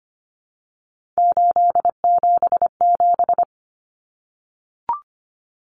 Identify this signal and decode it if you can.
Morse code, heard as a keyed tone, about 25 wpm, 700 Hz. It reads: 877
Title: Common Cut Numbers - 3-Digit 25wpm